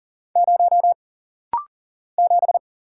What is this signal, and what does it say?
Morse code, heard as a keyed tone, about 40 wpm, 700 Hz. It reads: 0 7
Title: Single Letter-Number - Rapid-Fire 40wpm